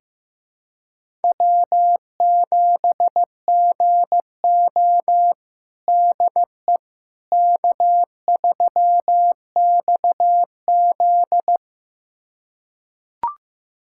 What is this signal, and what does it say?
Morse code, heard as a keyed tone, about 15 wpm, 700 Hz. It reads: W7GO DE K3XZ
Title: USA Callsign DE Callsign - 4 Characters - 15wpm